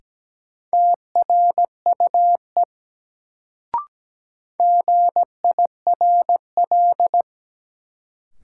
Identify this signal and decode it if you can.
Morse code, heard as a keyed tone, about 17 wpm, 700 Hz. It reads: TRUE GIRL
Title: Sentences from Top 400 Words - New Word Drill - 17wpm